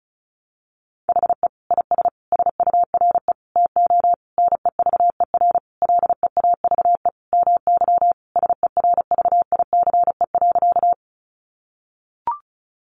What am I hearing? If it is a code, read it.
Morse code, heard as a keyed tone, about 34 wpm, 700 Hz. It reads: HE IS SURE TO NEVER LEAVE MY SERVICE.
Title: Sentences from Top 700 Words 35wpm